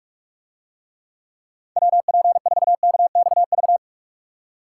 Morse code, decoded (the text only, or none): WP4KXV